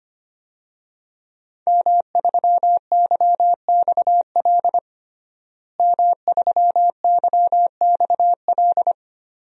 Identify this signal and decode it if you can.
Morse code, heard as a keyed tone, about 25 wpm, 700 Hz. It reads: M3YXL M3YXL
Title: DXCC-100 Call Signs - 5 Characters - Repeated 2x - 25wpm